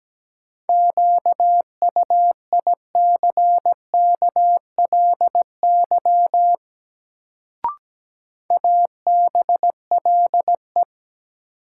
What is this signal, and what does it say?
Morse code, heard as a keyed tone, about 17 wpm, 700 Hz. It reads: QUICKLY ABLE
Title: Sentences from Top 700 Words - New Word Drill - 17wpm